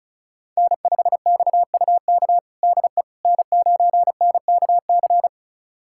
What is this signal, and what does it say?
Morse code, heard as a keyed tone, about 36 wpm, 700 Hz. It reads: N5XUK DE N9NKC
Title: USA Callsign DE Callsign - 5 Characters - 35wpm